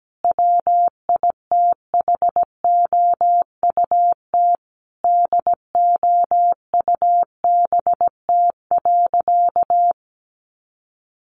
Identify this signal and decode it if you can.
Morse code, heard as a keyed tone, about 17 wpm, 700 Hz. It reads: WITHOUT DOUBT.
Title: Sentences from Top 1500 Words 17wpm